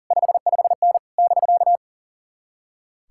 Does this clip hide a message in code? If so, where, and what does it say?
Yes: Morse code from 0.1 to 1.8 s, heard as a keyed tone, about 52 wpm, 700 Hz. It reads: 55N <BK>